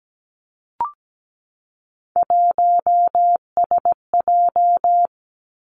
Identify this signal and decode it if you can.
Morse code, heard as a keyed tone, about 17 wpm, 700 Hz. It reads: 1SJ
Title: Three Challenging Letter-Number Combinations 17wpm